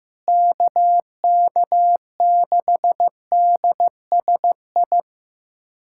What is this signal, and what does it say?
Morse code, heard as a keyed tone, about 15 wpm, 700 Hz. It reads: KK6DSI
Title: USA Callsign DE Callsign - 6 Characters - 15wpm